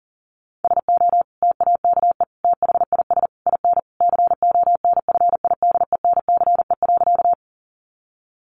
Morse code, heard as a keyed tone, about 40 wpm, 700 Hz. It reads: SO TAKE THIS IN CONFIDENCE.
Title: Sentences from Top 1500 Words 40wpm